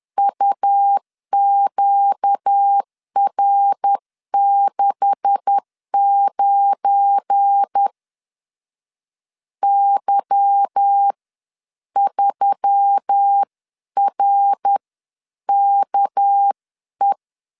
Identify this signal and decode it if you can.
Morse code, heard as a keyed tone, about 10 wpm, spaced as for 9 wpm, 800 Hz. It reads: UQR69 Y3RKE
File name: LEZIONI / 9 2867 10wpm 2733